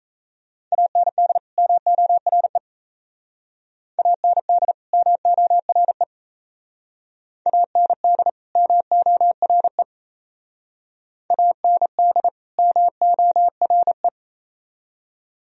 Morse code, heard as a keyed tone, about 34 wpm, 700 Hz. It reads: AND MORE AND MORE AND MORE AND MORE
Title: Speed-Racing - Sets of 2 Words 28wpm